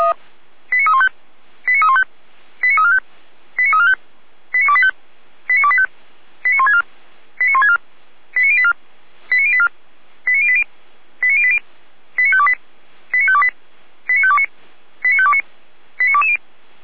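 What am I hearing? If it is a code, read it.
Five-tone selective call (ZVEI), heard as quick runs of five tones, tones of about 70 ms. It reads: 89315 89315 89345 89345 89285 89285 89164 89164 89084 89084 89080 89080 89528 89528 89529 89529 89220
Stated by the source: ZVEI five-tone call 89126 received by SDR.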